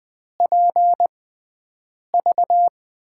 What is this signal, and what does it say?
Morse code, heard as a keyed tone, about 20 wpm, spaced as for 7 wpm, 700 Hz. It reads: PV
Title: USA Call Signs - 6x Character Spacing - 20wpm